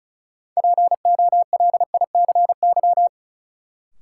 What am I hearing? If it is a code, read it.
Morse code, heard as a keyed tone, about 36 wpm, 700 Hz. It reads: POLICY
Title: Sentences from Top 1200 Words - New Word Drill Alt - 35wpm